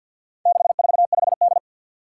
Morse code, heard as a keyed tone, about 53 wpm, 700 Hz. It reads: 645D
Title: Four Challenging Letter-Number Combinations 50wpm